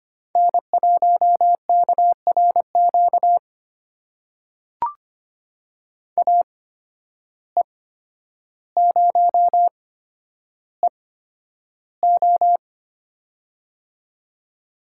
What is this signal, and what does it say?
Morse code, heard as a keyed tone, about 25 wpm, spaced as for 11 wpm, 700 Hz. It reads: N1KRQ AE0EO
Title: USA Call Signs - 5 Characters - 8x Character Spacing - 25wpm